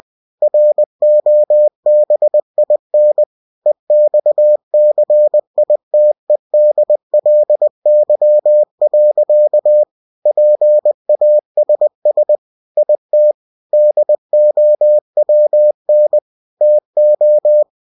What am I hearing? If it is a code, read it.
Morse code, heard as a keyed tone, about 20 wpm, 600 Hz. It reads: ROBIN EXCITEDLY. PASS IT DOWN TO
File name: cw-books / winnie-the-pooh / winnie-20wpm-page-0108